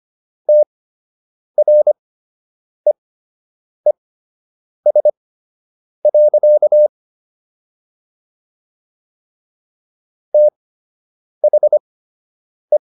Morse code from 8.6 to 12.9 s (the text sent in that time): THE